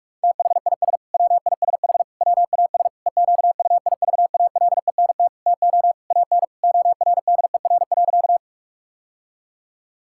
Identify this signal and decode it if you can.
Morse code, heard as a keyed tone, about 42 wpm, 700 Hz. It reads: THIS WISH WAS EQUIVALENT TO AN ORDER.
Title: Sentences from Top 1600 Words 45wpm